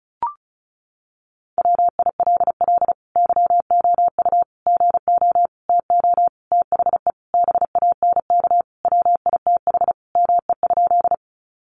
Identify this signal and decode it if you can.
Morse code, heard as a keyed tone, about 34 wpm, 700 Hz. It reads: WILL YOU GO TO THE BANK WITH ME?
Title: Sentences from Top 900 Words 35wpm